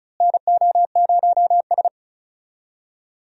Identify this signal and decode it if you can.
Morse code, heard as a keyed tone, about 34 wpm, 700 Hz. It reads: NO0S
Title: USA Call Signs - 4 Characters - Repeated 3x - 35wpm